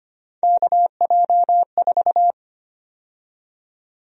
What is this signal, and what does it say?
Morse code, heard as a keyed tone, about 25 wpm, 700 Hz. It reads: KJ4
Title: USA Call Sign Prefixes - 25wpm